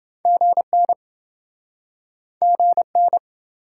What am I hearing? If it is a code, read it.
Morse code, heard as a keyed tone, about 27 wpm, 700 Hz. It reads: GN GN